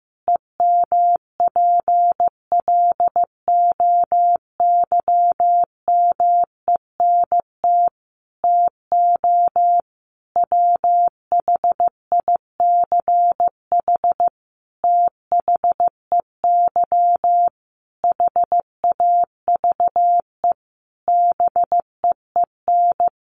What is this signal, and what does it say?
Morse code, heard as a keyed tone, about 15 wpm, 700 Hz. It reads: EMPLOYMENT TO WHICH THEY HAVE BEEN
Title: Sentences from Top 1500 Words 15wpm